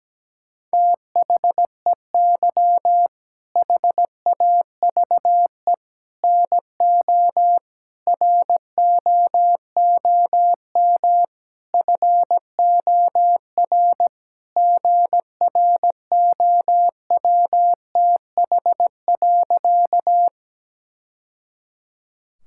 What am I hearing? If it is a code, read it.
Morse code, heard as a keyed tone, about 17 wpm, 700 Hz. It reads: THEY HAVE NO ROOM FOR GROWTH.
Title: Sentences from Top 1000 Words - Review Part 5 - 17wpm